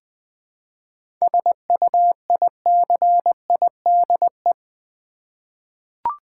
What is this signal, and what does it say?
Morse code, heard as a keyed tone, about 20 wpm, 700 Hz. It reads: SUICIDE
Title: 7 Letter Words - First 3 Letters Given - 20wpm